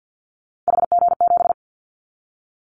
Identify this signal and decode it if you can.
Morse code, heard as a keyed tone, about 50 wpm, 700 Hz. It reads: HG7
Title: Callsign Prefixes for Top 50 DXCC - 50wpm